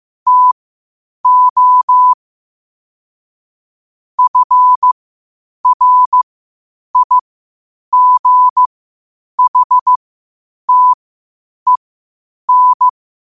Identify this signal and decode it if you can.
Morse code, heard as a keyed tone, about 15 wpm, spaced as for 8 wpm, 1000 Hz. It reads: TO FRIGHTEN
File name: qotdfiles / 8wpm